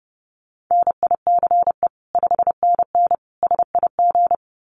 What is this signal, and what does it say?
Morse code, heard as a keyed tone, about 30 wpm, 700 Hz. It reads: NICE 5NN SIG
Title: Common QSO Phrases 30wpm